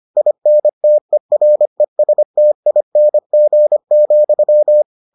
Morse code, heard as a keyed tone, about 25 wpm, 600 Hz. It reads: INTERESTING,